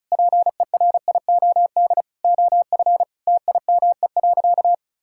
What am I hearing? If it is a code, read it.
Morse code, heard as a keyed tone, about 36 wpm, 700 Hz. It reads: PERIOD OF TIME.
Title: Sentences from Top 1000 Words - Review Part 5 - 35wpm